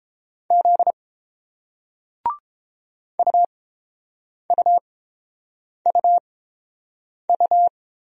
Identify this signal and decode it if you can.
Morse code, heard as a keyed tone, about 30 wpm, spaced as for 7 wpm, 700 Hz. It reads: Z UUUU